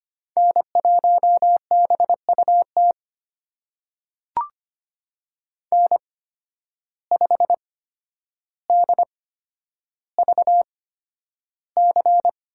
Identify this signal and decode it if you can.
Morse code, heard as a keyed tone, about 25 wpm, spaced as for 11 wpm, 700 Hz. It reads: N1BUT N5DVC
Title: USA Call Signs - 5 Characters - 8x Character Spacing - 25wpm